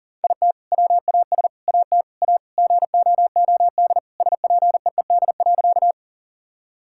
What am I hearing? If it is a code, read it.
Morse code, heard as a keyed tone, about 37 wpm, 700 Hz. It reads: IT WAS AT A GOOD SPEED.
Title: Sentences from Top 1000 Words - Review Part 5 - 40wpm